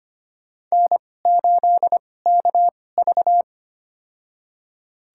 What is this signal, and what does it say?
Morse code, heard as a keyed tone, about 25 wpm, spaced as for 18 wpm, 700 Hz. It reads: N8KV
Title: USA Call Signs - 4 Characters - 2x Character Spacing - 25wpm